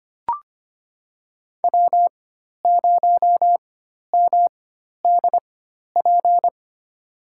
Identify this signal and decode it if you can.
Morse code, heard as a keyed tone, about 25 wpm, spaced as for 12 wpm, 700 Hz. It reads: W0MDP
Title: USA Call Signs - 5 Characters - 4x Character Spacing - 25wpm